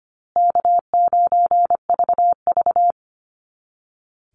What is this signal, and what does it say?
Morse code, heard as a keyed tone, about 25 wpm, 700 Hz. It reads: K9VV
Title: Easy Call Signs Repeated 3x - 25wpm